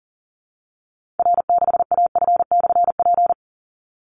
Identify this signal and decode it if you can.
Morse code, heard as a keyed tone, about 40 wpm, 700 Hz. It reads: R6AF/P
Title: Call Signs Repeated 3x - 40wpm